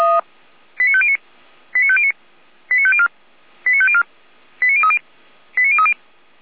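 Five-tone selective call (ZVEI), heard as quick runs of five tones, tones of about 70 ms. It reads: 89509 89509 89504 89504 89030 89030
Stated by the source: ZVEI five-tone selective call received on an SDR.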